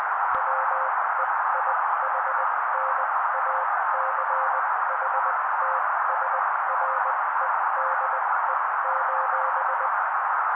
Morse code, heard as a keyed tone, about 20 wpm, spaced as for 16 wpm, 530 Hz. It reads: WEIHNACHTSREDE:
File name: daily / 16wpm noise